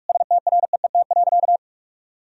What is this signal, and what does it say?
Morse code, heard as a keyed tone, about 41 wpm, 700 Hz. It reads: STREET.
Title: Sentences from Top 1000 Words - Review Part 3 - 45wpm